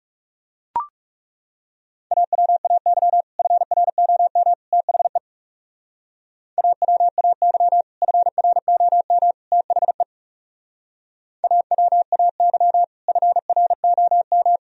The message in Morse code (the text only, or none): AWAY FROM THE AWAY FROM THE AWAY FROM